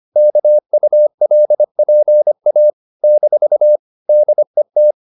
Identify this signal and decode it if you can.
Morse code, heard as a keyed tone, about 25 wpm, 600 Hz. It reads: KULPA - DET
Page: Ferie 25 wpm